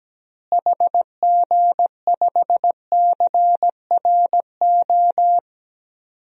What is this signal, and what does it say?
Morse code, heard as a keyed tone, about 17 wpm, 700 Hz. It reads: HG5CRO
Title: DXCC-100 Call Signs - 6 Characters - Repeated 3x - 17wpm